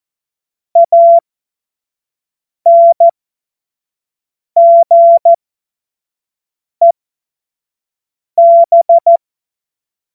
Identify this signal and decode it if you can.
Morse code, heard as a keyed tone, about 14 wpm, spaced as for 5 wpm, 680 Hz. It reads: ANGEB
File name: Morsekurs / Witze / Witz 17. Januar 2023 25 F70